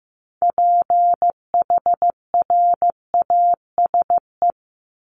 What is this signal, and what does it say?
Morse code, heard as a keyed tone, about 15 wpm, 700 Hz. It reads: PHRASE
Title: Top 1501-1750 Words 15wpm